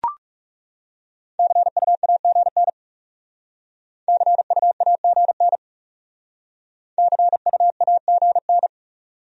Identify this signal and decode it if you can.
Morse code, heard as a keyed tone, about 38 wpm, 700 Hz. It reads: CUAGN CUAGN CUAGN